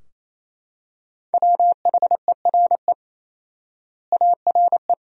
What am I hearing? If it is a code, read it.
Morse code, heard as a keyed tone, about 28 wpm, 700 Hz. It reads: WHERE ARE
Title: 4x Word-Spacing - Sentences from Top 200 Words - 28wpm